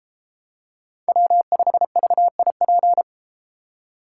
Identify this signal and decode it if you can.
Morse code, heard as a keyed tone, about 33 wpm, 700 Hz. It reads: W5VIP